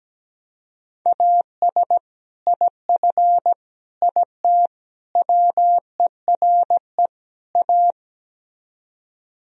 Morse code, heard as a keyed tone, about 17 wpm, 700 Hz. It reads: AS IF IT WERE A
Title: Speed-Racing - Sets of 5 Words 17wpm